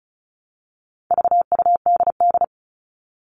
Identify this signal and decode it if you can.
Morse code, heard as a keyed tone, about 34 wpm, 700 Hz. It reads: VUDD
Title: All Cut Numbers - 4-Digit 35wpm